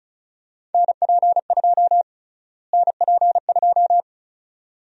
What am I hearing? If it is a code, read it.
Morse code, heard as a keyed tone, about 34 wpm, 700 Hz. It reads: NP2 NP2
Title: USA Call Sign Prefixes - Repeated 2x - 35wpm